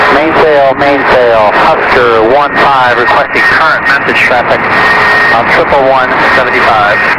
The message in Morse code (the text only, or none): none